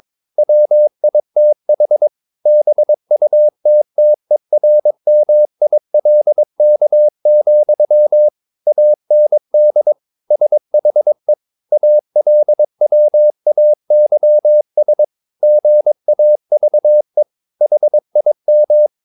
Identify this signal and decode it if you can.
Morse code, heard as a keyed tone, about 22 wpm, 600 Hz. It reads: WITH BUTTERMILK, AND SHE ALWAYS GAVE HIM
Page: Romanen Never af Ken Follet på engelsk.